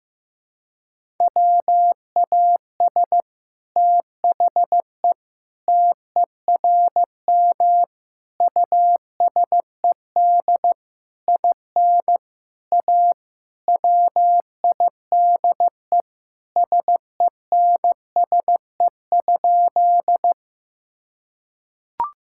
Morse code, heard as a keyed tone, about 15 wpm, 700 Hz. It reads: WAS THE TERM USED IN A WIDE SENSE?
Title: Sentences from Top 1000 Words - Review Part 6 - 15wpm